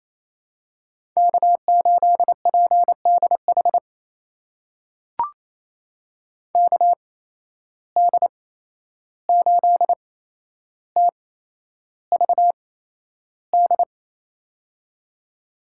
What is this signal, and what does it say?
Morse code, heard as a keyed tone, about 27 wpm, spaced as for 11 wpm, 700 Hz. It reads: K8PDH KD8TVD
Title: USA Call Signs - 8x Character Spacing - 28wpm